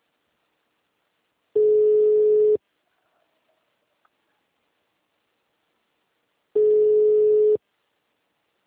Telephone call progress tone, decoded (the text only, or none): ringback tone